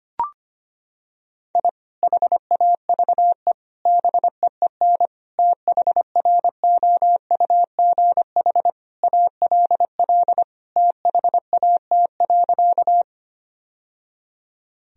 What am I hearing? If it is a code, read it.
Morse code, heard as a keyed tone, about 25 wpm, 700 Hz. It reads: I HAVE BEEN THROUGH ALL THAT.
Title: Sentences from Top 1000 Words - Review Part 1 - 25wpm